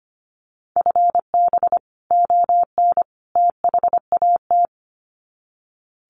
Morse code, heard as a keyed tone, about 25 wpm, 700 Hz. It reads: FB ON THAT
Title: Common QSO Phrases 25wpm